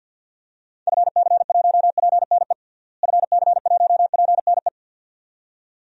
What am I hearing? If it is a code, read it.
Morse code, heard as a keyed tone, about 51 wpm, 700 Hz. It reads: FC1PNE FC1PNE
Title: DXCC-100 Call Signs - 6 Characters - Repeated 2x - 50wpm